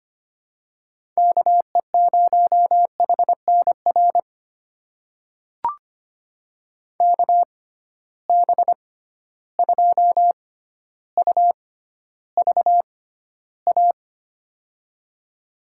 Morse code, heard as a keyed tone, about 25 wpm, spaced as for 13 wpm, 700 Hz. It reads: KE0HNR KB2UVA
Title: USA Call Signs - 6 Characters - 6x Character Spacing - 25wpm